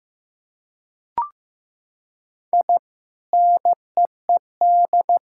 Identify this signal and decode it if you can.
Morse code, heard as a keyed tone, about 15 wpm, 700 Hz. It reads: I NEED
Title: Sentences from Top 1600 Words 15wpm